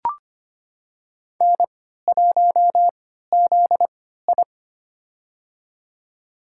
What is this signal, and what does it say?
Morse code, heard as a keyed tone, about 25 wpm, spaced as for 14 wpm, 700 Hz. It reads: N1ZI
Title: USA Call Signs - 4 Characters - 3x Character Spacing - 25wpm